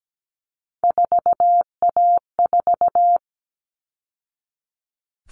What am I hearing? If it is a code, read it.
Morse code, heard as a keyed tone, about 17 wpm, 700 Hz. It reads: 4A4
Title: All Cut Numbers - 3-Digit 17wpm